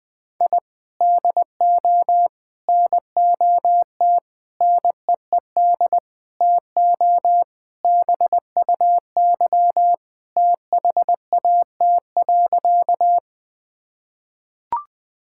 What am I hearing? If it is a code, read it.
Morse code, heard as a keyed tone, about 21 wpm, 700 Hz. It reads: I DO NOT NEED TO BUY THAT.
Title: Sentences with 5-Word Phrases - 20wpm